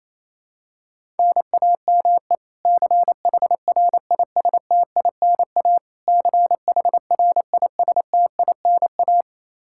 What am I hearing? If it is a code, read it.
Morse code, heard as a keyed tone, about 27 wpm, 700 Hz. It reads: NAME CHRISTINA CHRISTINA
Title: K1USN SST QSO Practice 28wpm